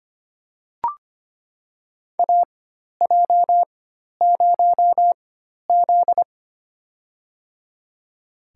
Morse code, heard as a keyed tone, about 25 wpm, spaced as for 12 wpm, 700 Hz. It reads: AJ0Z
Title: USA Call Signs - 4 Characters - 4x Character Spacing - 25wpm